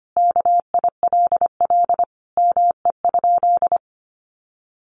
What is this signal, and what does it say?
Morse code, heard as a keyed tone, about 25 wpm, 700 Hz. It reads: KILL ME?
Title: Sentences from Top 1400 Words 25wpm